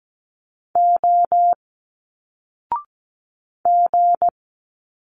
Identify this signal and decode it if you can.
Morse code, heard as a keyed tone, about 17 wpm, 700 Hz. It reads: O G
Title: Single Letters 17wpm